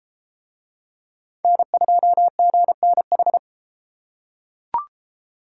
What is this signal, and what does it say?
Morse code, heard as a keyed tone, about 32 wpm, 700 Hz. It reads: N2GNH